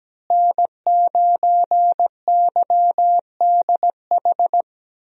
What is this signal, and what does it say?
Morse code, heard as a keyed tone, about 17 wpm, 700 Hz. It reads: N9YDH
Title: USA Call Signs - 5 Characters - Repeated 3x - 17wpm